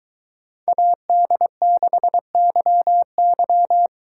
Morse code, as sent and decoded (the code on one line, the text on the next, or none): .- -.. -.... -.-- -.--
AD6YY